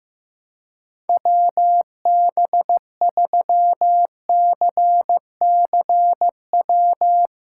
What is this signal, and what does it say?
Morse code, heard as a keyed tone, about 15 wpm, 700 Hz. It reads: WB3CCW
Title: USA Callsign DE Callsign - 6 Characters - 15wpm